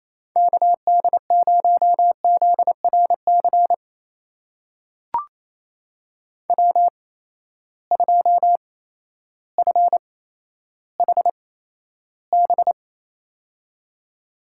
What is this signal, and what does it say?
Morse code, heard as a keyed tone, about 28 wpm, spaced as for 13 wpm, 700 Hz. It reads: KD0ZRC W2FHB
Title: USA Call Signs - 8x Character Spacing - 28wpm